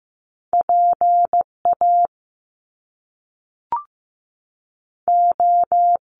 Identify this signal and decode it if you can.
Morse code, heard as a keyed tone, about 15 wpm, 700 Hz. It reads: PA O
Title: Top Two Letter Combinations in Words 15wpm